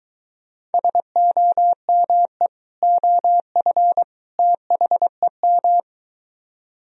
Morse code, heard as a keyed tone, about 23 wpm, 700 Hz. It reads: SOME OF THEM